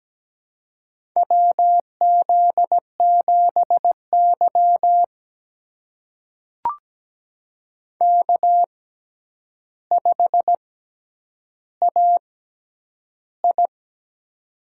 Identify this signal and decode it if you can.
Morse code, heard as a keyed tone, about 17 wpm, spaced as for 9 wpm, 700 Hz. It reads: WZ7Y K5AI
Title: USA Call Signs - 4 Characters - 6x Character Spacing - 17wpm